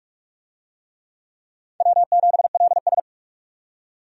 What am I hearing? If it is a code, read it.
Morse code, heard as a keyed tone, about 46 wpm, 700 Hz. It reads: W7LS
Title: USA Call Signs - 4 Characters - 45wpm